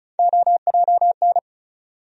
Morse code, heard as a keyed tone, about 34 wpm, 700 Hz. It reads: OJN
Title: USA Call Sign Postfixes - 3 Letters 35wpm